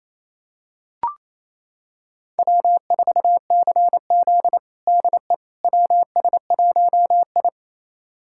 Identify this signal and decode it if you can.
Morse code, heard as a keyed tone, about 28 wpm, 700 Hz. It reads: W4CZ DE WS1I
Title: USA Callsign DE Callsign - 4 Characters - 28wpm